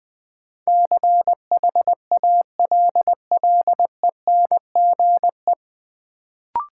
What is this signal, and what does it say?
Morse code, heard as a keyed tone, about 20 wpm, 700 Hz. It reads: CHALLENGE
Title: Sentences from Top 1000 Words - Review Part 7 - New Word Drill - 20wpm